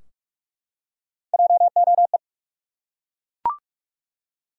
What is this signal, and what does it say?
Morse code, heard as a keyed tone, about 41 wpm, 700 Hz. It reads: JOE
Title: USA Common Names - Max 3 Letters - 45wpm